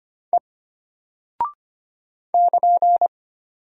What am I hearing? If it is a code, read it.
Morse code, heard as a keyed tone, about 25 wpm, 700 Hz. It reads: E <KN>
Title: Single Character (Letter:Number:Punctuation:Prosign) Alt 25wpm